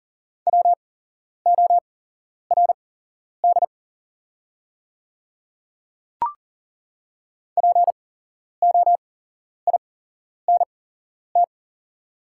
Morse code, heard as a keyed tone, about 40 wpm, spaced as for 11 wpm, 700 Hz. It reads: WORD POINT
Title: 8x Character Spacing - Top 200 Words - 40wpm